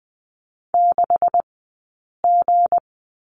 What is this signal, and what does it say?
Morse code, heard as a keyed tone, about 20 wpm, 700 Hz. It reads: 6 G
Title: 190 Lesson 31d - Letter v - Two Characters 20wpm